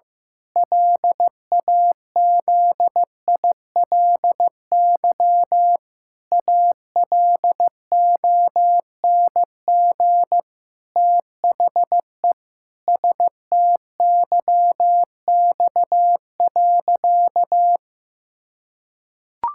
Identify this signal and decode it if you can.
Morse code, heard as a keyed tone, about 15 wpm, 700 Hz. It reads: LAZILY ALONG THE STYX.
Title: Pangram Challenge v2 - 15wpm